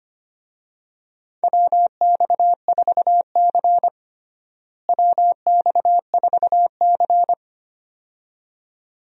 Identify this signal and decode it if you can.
Morse code, heard as a keyed tone, about 25 wpm, 700 Hz. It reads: WX4C WX4C
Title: USA Call Signs - 4 Characters - Repeated 2x - 25wpm